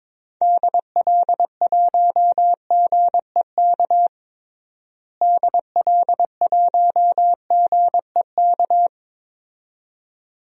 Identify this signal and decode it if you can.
Morse code, heard as a keyed tone, about 22 wpm, 700 Hz. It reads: DL1GEK DL1GEK
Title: DXCC-100 Call Signs - 6 Characters - Repeated 2x - 22wpm